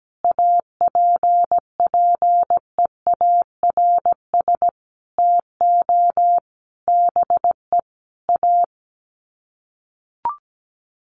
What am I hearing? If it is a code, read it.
Morse code, heard as a keyed tone, about 17 wpm, 700 Hz. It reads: APPEARS TO BE A
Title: Sets of 5 Words Encore2 17wpm